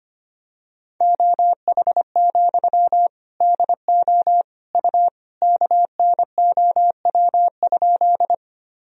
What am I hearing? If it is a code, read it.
Morse code, heard as a keyed tone, about 25 wpm, 700 Hz. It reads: OH, DO U KNOW?
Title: Sentences from Top 500 Words and Abbreviations 25wpm